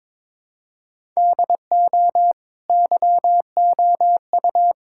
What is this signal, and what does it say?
Morse code, heard as a keyed tone, about 22 wpm, 700 Hz. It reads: DO YOU
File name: practice / 22wpm / Sentences with 2-Word Phrases - 22wpm